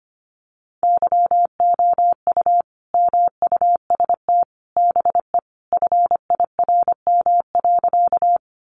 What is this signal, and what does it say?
Morse code, heard as a keyed tone, about 25 wpm, 700 Hz. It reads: YOU MUST BE FIRM.
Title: Sentences from Top 1400 Words 25wpm